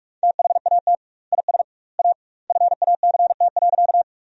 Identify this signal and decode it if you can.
Morse code, heard as a keyed tone, about 46 wpm, 700 Hz. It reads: THAT IS A FACT.
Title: 2x Word-Spacing - Sentences from Top 500 Words - 45wpm